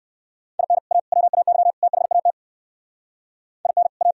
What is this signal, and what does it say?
Morse code, heard as a keyed tone, about 68 wpm, 700 Hz. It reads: IN A WAY THAT IN A